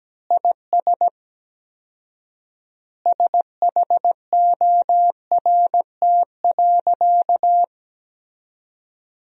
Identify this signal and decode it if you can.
Morse code, heard as a keyed tone, about 17 wpm, 700 Hz. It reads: IS SHORT.
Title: 4x Word-Spacing - Sentences from Top 500 Words - 17wpm